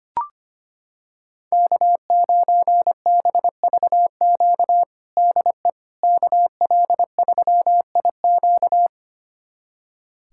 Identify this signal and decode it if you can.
Morse code, heard as a keyed tone, about 25 wpm, 700 Hz. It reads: K9BVQ DE KL3IQ
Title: USA Callsign DE Callsign - 5 Characters - 25wpm